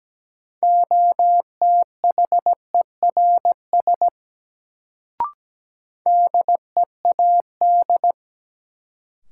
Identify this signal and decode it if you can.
Morse code, heard as a keyed tone, about 17 wpm, 700 Hz. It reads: OTHERS DEAD
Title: Sentences from Top 1500 Words - New Word Drill Alt - 17wpm